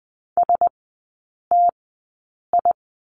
Morse code, heard as a keyed tone, about 20 wpm, spaced as for 8 wpm, 700 Hz. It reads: STI
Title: 067 Lesson 12e - Letter d - Three Characters 20wpm